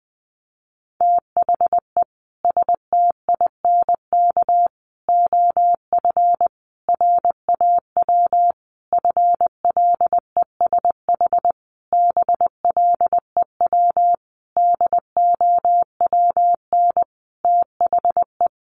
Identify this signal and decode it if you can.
Morse code, heard as a keyed tone, about 20 wpm, 700 Hz. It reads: THE STINK OF RAW FLESH BLEW DOWN THE
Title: A Sound Of Thunder 20wpm